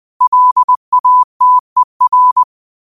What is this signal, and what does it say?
Morse code, heard as a keyed tone, about 20 wpm, 1000 Hz. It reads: LATER